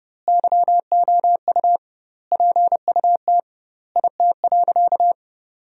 Morse code, heard as a keyed tone, about 30 wpm, 700 Hz. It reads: YOU PUT IT.
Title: 2x Word-Spacing - Sentences from Top 200 Words - 30wpm